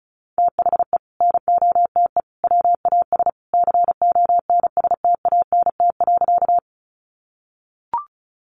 Morse code, heard as a keyed tone, about 34 wpm, 700 Hz. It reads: THE NOTE WAS CONSTANT.
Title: Sentences from Top 1500 Words 35wpm